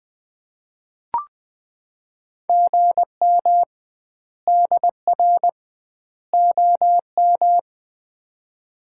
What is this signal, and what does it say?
Morse code, heard as a keyed tone, about 20 wpm, 700 Hz. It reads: GM DR OM
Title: Common QSO Phrases - 2x Word-Spacing - 20wpm